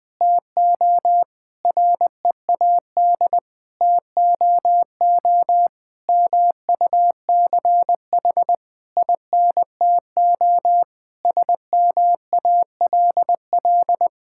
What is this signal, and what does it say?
Morse code, heard as a keyed tone, about 20 wpm, 700 Hz. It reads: TO READ TOO MUCH INTO SMALL
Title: Sentences with 9-Word Phrases - Spoken Beginning - 20wpm